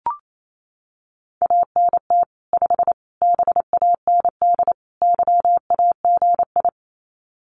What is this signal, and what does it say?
Morse code, heard as a keyed tone, about 28 wpm, 700 Hz. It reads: ANT 5 BAND YAGI
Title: Common QSO Phrases 28wpm